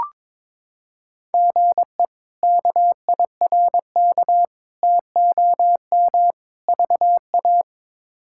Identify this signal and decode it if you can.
Morse code, heard as a keyed tone, about 21 wpm, 700 Hz. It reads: GE KIRK TOM VA
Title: K1USN SST QSO Practice - 22wpm